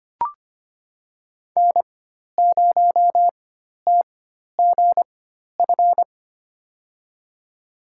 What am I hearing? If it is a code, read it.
Morse code, heard as a keyed tone, about 25 wpm, spaced as for 12 wpm, 700 Hz. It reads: N0TGF
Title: USA Call Signs - 4x Character Spacing - 25wpm